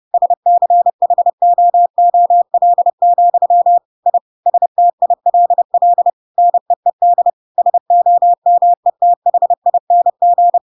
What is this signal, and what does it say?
Morse code, heard as a keyed tone, about 30 wpm, 700 Hz. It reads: SCHOOL, I STILL NEED SOMETHING
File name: Pinocchio / Pinocchio ch8 30 30 ik1hge